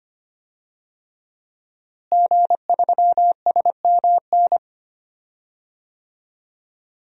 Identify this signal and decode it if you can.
Morse code, heard as a keyed tone, about 25 wpm, 700 Hz. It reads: G3SMN
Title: DXCC-100 Call Signs - 5 Characters - Repeated 3x - 25wpm